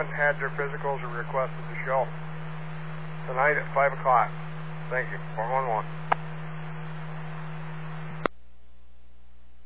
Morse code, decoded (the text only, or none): none